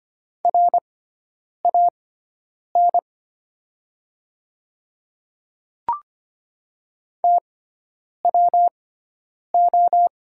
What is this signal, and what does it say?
Morse code, heard as a keyed tone, about 25 wpm, spaced as for 9 wpm, 700 Hz. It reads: RAN TWO
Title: 6x Character Spacing - Top 500 Words - 25wpm